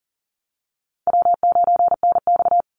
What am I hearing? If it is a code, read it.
Morse code, heard as a keyed tone, about 40 wpm, 700 Hz. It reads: W9NX
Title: Easy Call Signs - 40wpm